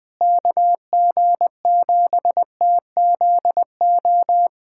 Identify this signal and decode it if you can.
Morse code, heard as a keyed tone, about 20 wpm, 700 Hz. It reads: KG7TZO
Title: USA Callsign DE Callsign - 6 Characters - 20wpm